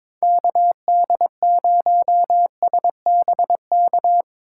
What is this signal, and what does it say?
Morse code, heard as a keyed tone, about 22 wpm, 700 Hz. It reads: KD0SBK
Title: USA Call Signs - 6 Characters - 4x Character Spacing - 22wpm